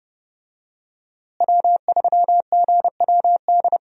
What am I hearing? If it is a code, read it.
Morse code, heard as a keyed tone, about 29 wpm, 700 Hz. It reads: W3GWD
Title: USA Call Signs - 5 Characters - Repeated 2x - 30wpm